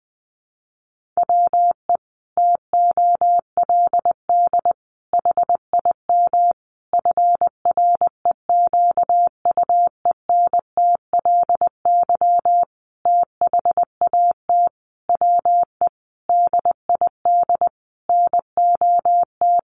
Morse code, heard as a keyed tone, about 20 wpm, 700 Hz. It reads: WE TOLD HIM FREQUENTLY THAT WE DID NOT
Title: Sentences from Top 1300 Words 20wpm